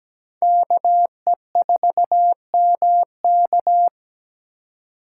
Morse code, heard as a keyed tone, about 20 wpm, 700 Hz. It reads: KE4MK
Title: Speed Racing - USA 2x2 Call Signs 17wpm